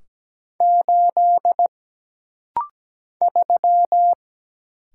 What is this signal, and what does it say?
Morse code, heard as a keyed tone, about 17 wpm, 700 Hz. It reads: 8 3
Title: Single Letter-Number Alt - Rapid-Fire 17wpm